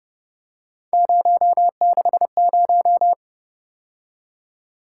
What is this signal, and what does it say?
Morse code, heard as a keyed tone, about 31 wpm, 700 Hz. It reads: 060
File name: Speed-Racing - Three Letter-Number Combinations - 25wpm